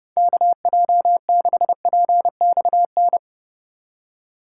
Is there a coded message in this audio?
Yes: Morse code from 0.2 to 3.2 s, heard as a keyed tone, about 30 wpm, 700 Hz. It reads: KJ6PXN